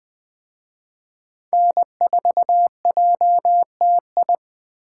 Morse code, heard as a keyed tone, about 21 wpm, 700 Hz. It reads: N4JTI
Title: Speed Racing - USA 1x3 Call Signs 17wpm